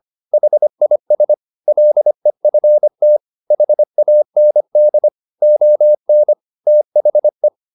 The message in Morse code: HIS LEFT HAND ON THE

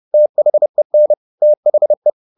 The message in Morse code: THEN THE

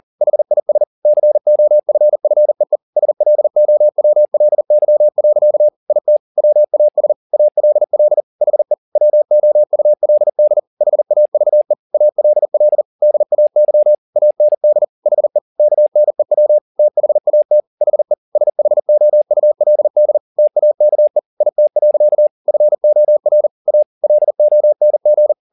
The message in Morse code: HIS COFFEE SLOWLY. IT WAS ALL HE WOULD HAVE ALL DAY AND HE KNEW THAT HE SHOULD TAKE IT. FOR A LONG

